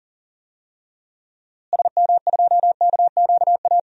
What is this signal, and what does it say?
Morse code, heard as a keyed tone, about 40 wpm, 700 Hz. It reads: SM2KQA
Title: DXCC-100 Call Signs - 6 Characters - Repeated 3x - 40wpm